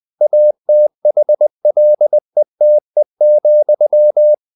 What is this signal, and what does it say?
Morse code, heard as a keyed tone, about 20 wpm, 600 Hz. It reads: ATHLETE,